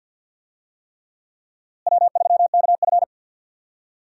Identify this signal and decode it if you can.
Morse code, heard as a keyed tone, about 52 wpm, 700 Hz. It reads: W3KF
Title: USA Call Signs - 4 Characters - Repeated 3x - 50wpm